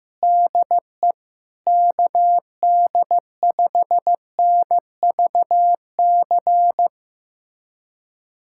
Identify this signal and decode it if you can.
Morse code, heard as a keyed tone, about 15 wpm, 700 Hz. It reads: DE KD5NVC
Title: USA Callsign DE Callsign - 6 Characters - 15wpm